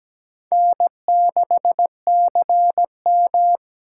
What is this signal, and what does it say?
Morse code, heard as a keyed tone, about 20 wpm, 700 Hz. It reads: N6CM
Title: Speed Racing - USA 1x2 Call Signs 17wpm